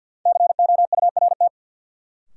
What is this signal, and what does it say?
Morse code, heard as a keyed tone, about 50 wpm, 700 Hz. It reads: COURT
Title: Sentences from Top 1600 Words - New Word Drill Alt - 50wpm